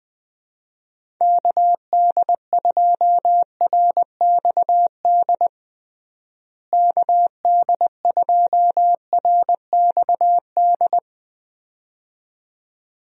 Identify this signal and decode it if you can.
Morse code, heard as a keyed tone, about 20 wpm, 700 Hz. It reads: KD2RXD KD2RXD
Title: USA Call Signs - 6 Characters - Repeated 2x - 20wpm